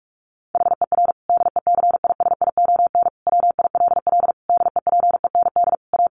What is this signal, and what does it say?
Morse code, heard as a keyed tone, about 44 wpm, 700 Hz. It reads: HER DECISION WILL DEPEND A
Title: Sentences from Top 1300 Words 45wpm